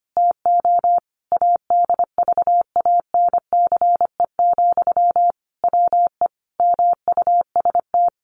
Morse code, heard as a keyed tone, about 25 wpm, 700 Hz. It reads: TO ADVANCE, WE MUST
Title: Sentences from Top 1400 Words 25wpm